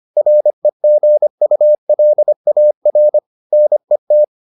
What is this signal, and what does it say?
Morse code, heard as a keyed tone, about 25 wpm, 600 Hz. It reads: REGULAR NET